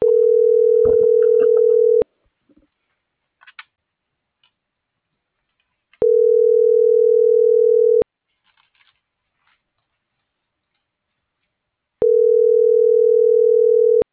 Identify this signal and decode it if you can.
Ringback tone (North America), 440 + 480 Hz.